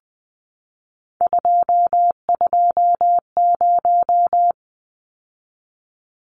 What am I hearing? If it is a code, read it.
Morse code, heard as a keyed tone, about 20 wpm, 700 Hz. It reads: 220
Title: Numbers - 3-Digit 20wpm